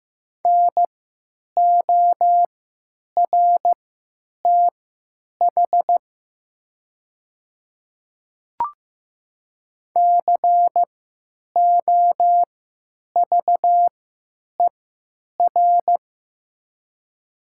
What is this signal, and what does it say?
Morse code, heard as a keyed tone, about 15 wpm, spaced as for 9 wpm, 700 Hz. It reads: NORTH COVER
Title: 3x Character Spacing - Top 300 Words - 15wpm